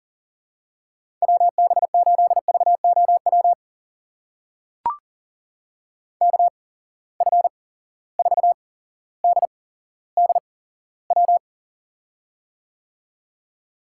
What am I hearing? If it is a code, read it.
Morse code, heard as a keyed tone, about 41 wpm, spaced as for 17 wpm, 700 Hz. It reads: WB8VOW KF4DDW
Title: USA Call Signs - 6 Characters - 8x Character Spacing - 40wpm